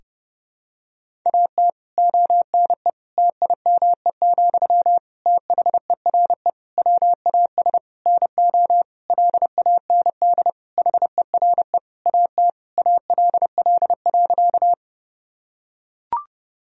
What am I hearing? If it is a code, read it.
Morse code, heard as a keyed tone, about 30 wpm, 700 Hz. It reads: AT ONE TIME, THERE WAS NO LAND HERE AT ALL.
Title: Sentences from Top 1000 Words - Review Part 1 - 30wpm